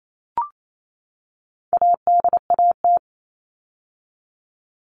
Morse code, heard as a keyed tone, about 28 wpm, 700 Hz. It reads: ADAT
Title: All Cut Numbers - 4-Digit 28wpm